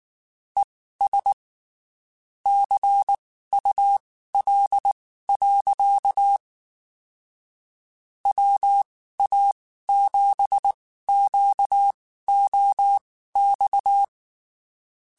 Morse code, heard as a keyed tone, about 19 wpm, spaced as for 14 wpm, 800 Hz. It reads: ES CUL. WA7QOX
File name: (spaced) / QSOs Spaced-mp3s / WA7QOX 2 @18 sp